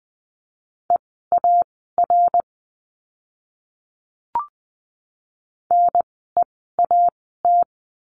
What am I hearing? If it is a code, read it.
Morse code, heard as a keyed tone, about 20 wpm, spaced as for 14 wpm, 700 Hz. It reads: EAR NEAT
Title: 054 Lesson 10f - Letter r - Words 20wpm